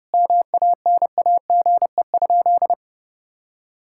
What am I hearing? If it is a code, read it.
Morse code, heard as a keyed tone, about 29 wpm, 700 Hz. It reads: MANAGE?
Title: Sentences from Top 1600 Words 30wpm